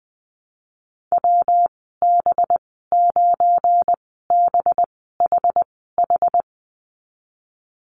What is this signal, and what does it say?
Morse code, heard as a keyed tone, about 20 wpm, spaced as for 15 wpm, 700 Hz. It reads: WB9BHH
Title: 213 Lesson 34g - Number 8 - Call Signs 20wpm